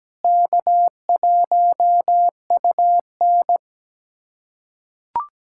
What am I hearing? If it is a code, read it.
Morse code, heard as a keyed tone, about 17 wpm, 700 Hz. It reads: K1UN
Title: USA Callsign DE Callsign - 4 Characters - 17wpm